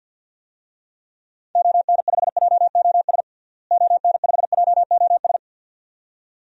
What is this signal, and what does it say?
Morse code, heard as a keyed tone, about 50 wpm, 700 Hz. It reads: ON5JOS ON5JOS
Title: DXCC-100 Call Signs - 6 Characters - Repeated 2x - 50wpm